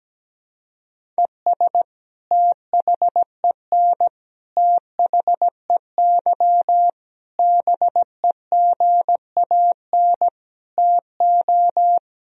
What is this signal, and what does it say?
Morse code, heard as a keyed tone, about 17 wpm, 700 Hz. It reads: ES THEN THEY BEGAN TO
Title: Sentences from Top 300 Words and Abbreviations 17wpm